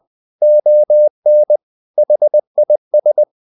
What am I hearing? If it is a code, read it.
Morse code, heard as a keyed tone, about 20 wpm, 600 Hz. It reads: ON HIS
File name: the old man and the sea part 2 20wpm